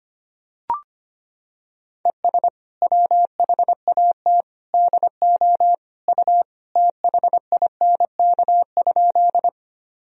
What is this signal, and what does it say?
Morse code, heard as a keyed tone, about 25 wpm, 700 Hz. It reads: ES WHAT DO U THINK?
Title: Sentences from Top 200 Words and Abbreviations 25wpm